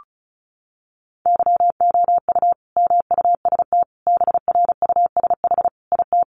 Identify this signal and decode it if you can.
Morse code, heard as a keyed tone, about 34 wpm, 700 Hz. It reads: YOU MUST BRUSH IT
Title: Sentences from Top 1100 Words 35wpm